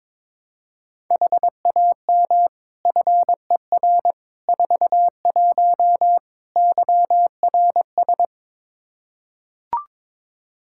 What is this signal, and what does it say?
Morse code, heard as a keyed tone, about 22 wpm, 700 Hz. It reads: HAM FER 41 YRS
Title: Practice QSO - Exchange 3 - More Info - 22wpm